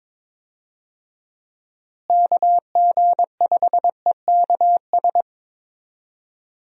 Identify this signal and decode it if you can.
Morse code, heard as a keyed tone, about 21 wpm, 700 Hz. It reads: KG5EKS
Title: USA Call Signs - 6 Characters - Repeated 2x - 22wpm